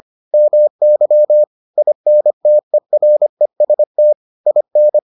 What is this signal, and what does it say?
Morse code, heard as a keyed tone, about 25 wpm, 600 Hz. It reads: MY INTEREST IN